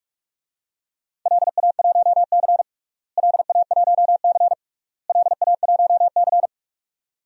Morse code, heard as a keyed tone, about 44 wpm, 700 Hz. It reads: LA1C LA1C LA1C
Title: DXCC-100 Call Signs - 4 Characters - Repeated 3x - 45wpm